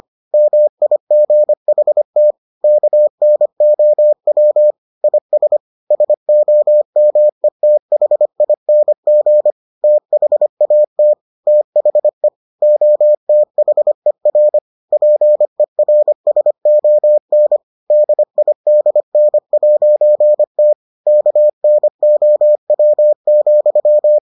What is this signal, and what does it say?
Morse code, heard as a keyed tone, about 25 wpm, 600 Hz. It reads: MIGHT KNOW IS SOMETHING THAT THE OTHER PERSON DIDN'T KNOW,